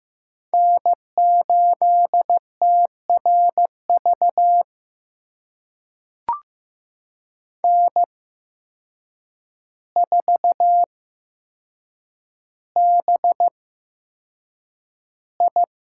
Morse code, heard as a keyed tone, about 15 wpm, spaced as for 7 wpm, 700 Hz. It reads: N8TRV N4BI
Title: USA Call Signs - 5 Characters - 8x Character Spacing - 15wpm